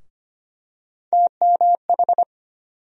Morse code, heard as a keyed tone, about 25 wpm, 700 Hz. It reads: TMH
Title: USA Call Sign Postfixes - 3 Letters 25wpm